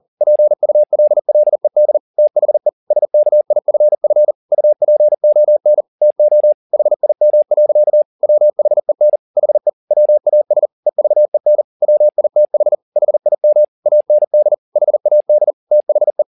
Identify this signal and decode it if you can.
Morse code, heard as a keyed tone, about 40 wpm, 600 Hz. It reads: PULLED THE SKIFF UP ONTO HIM. WHEN HE WAS EVEN WITH HIM AND HAD THE
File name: the old man and the sea 2 40wpm